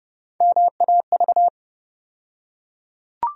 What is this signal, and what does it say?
Morse code, heard as a keyed tone, about 30 wpm, 700 Hz. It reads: MAV
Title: USA Call Sign Postfixes - 3 Letters 30wpm